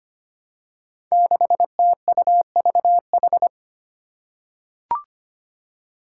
Morse code, heard as a keyed tone, about 25 wpm, 700 Hz. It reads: 6TUVH
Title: Five Challenging Letter-Number Combinations 25wpm